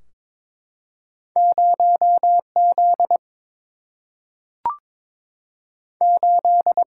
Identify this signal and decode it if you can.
Morse code, heard as a keyed tone, about 22 wpm, 700 Hz. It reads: 0 Z 8
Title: Single-Letter Single-Number 22wpm